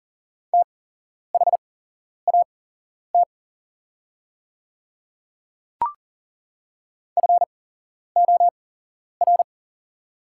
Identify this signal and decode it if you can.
Morse code, heard as a keyed tone, about 40 wpm, spaced as for 11 wpm, 700 Hz. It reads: THAT FOR
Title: 8x Character Spacing - Top 100 Words - 40wpm